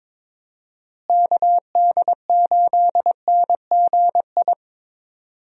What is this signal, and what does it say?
Morse code, heard as a keyed tone, about 22 wpm, 700 Hz. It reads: KD8NGI
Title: USA Call Signs - 6 Characters - Repeated 2x - 22wpm